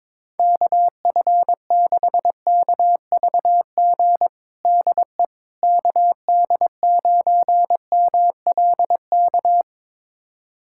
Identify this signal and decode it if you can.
Morse code, heard as a keyed tone, about 22 wpm, 700 Hz. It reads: KF6KVG DE KD9MLK
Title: USA Callsign DE Callsign 22wpm